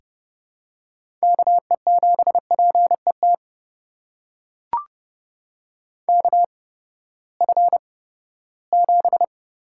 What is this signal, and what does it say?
Morse code, heard as a keyed tone, about 30 wpm, spaced as for 17 wpm, 700 Hz. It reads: KE7PET KF7
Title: USA Call Signs - 6 Characters - 8x Character Spacing - 30wpm